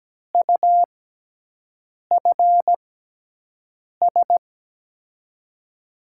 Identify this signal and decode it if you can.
Morse code, heard as a keyed tone, about 17 wpm, spaced as for 6 wpm, 700 Hz. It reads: UFS
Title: USA Call Signs - 5 Characters - 6x Character Spacing - 17wpm